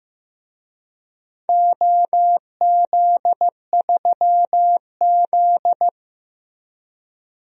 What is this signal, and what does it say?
Morse code, heard as a keyed tone, about 15 wpm, 700 Hz. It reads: OZ3Z
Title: DXCC-100 Call Signs - 4 Characters - Repeated 2x - 15wpm